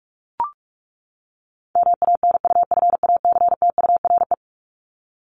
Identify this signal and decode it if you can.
Morse code, heard as a keyed tone, about 44 wpm, 700 Hz. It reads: MANUFACTURE
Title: ING Suffix - Top 100 Words with Base Words 45wpm